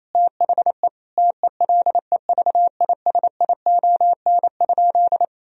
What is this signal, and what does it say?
Morse code, heard as a keyed tone, about 28 wpm, 700 Hz. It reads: THE TELEVISION?
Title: Sentences from Top 1000 Words - Review Part 4 - 28wpm